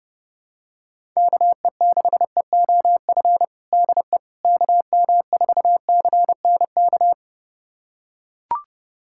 Morse code, heard as a keyed tone, about 30 wpm, 700 Hz. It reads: KE6EOF DE KM4CNK
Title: USA Callsign DE Callsign - 6 Characters - 30wpm